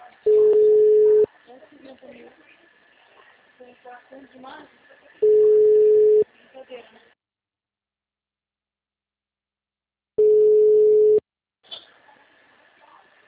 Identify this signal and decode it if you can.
Ringback tone (Europe, 425 Hz), with speech.